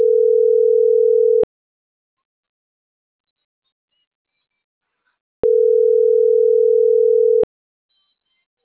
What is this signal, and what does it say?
Ringback tone (North America), 440 + 480 Hz.
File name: CENTRAL SAGOT / out pablo.sagotvv